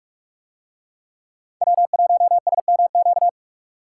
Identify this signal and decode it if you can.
Morse code, heard as a keyed tone, about 44 wpm, 700 Hz. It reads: W1SMQ